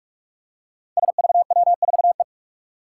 Morse code, heard as a keyed tone, about 45 wpm, 700 Hz. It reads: SVW4E
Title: Five Challenging Letter-Number Combinations 45wpm